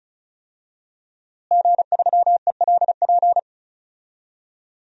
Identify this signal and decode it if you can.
Morse code, heard as a keyed tone, about 36 wpm, 700 Hz. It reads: G3ELP
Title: DXCC-100 Call Signs - 5 Characters - Repeated 3x - 35wpm